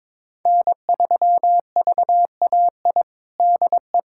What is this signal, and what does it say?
Morse code, heard as a keyed tone, about 22 wpm, 700 Hz. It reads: N3VAI DE
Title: USA Callsign DE Callsign 22wpm